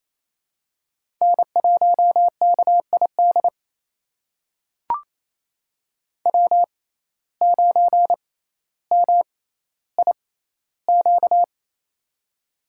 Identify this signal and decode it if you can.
Morse code, heard as a keyed tone, about 28 wpm, spaced as for 14 wpm, 700 Hz. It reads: N1KID W9MIQ
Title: USA Call Signs - 5 Characters - 6x Character Spacing - 28wpm